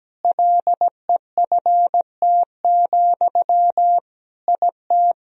Morse code, heard as a keyed tone, about 17 wpm, 700 Hz. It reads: LEFT, IT
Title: Sentences from Top 400 Words and Abbreviations 17wpm